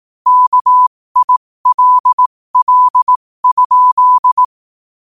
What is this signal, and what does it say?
Morse code, heard as a keyed tone, about 18 wpm, spaced as for 15 wpm, 1000 Hz. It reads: KILL?
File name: qotdfiles / 15wpm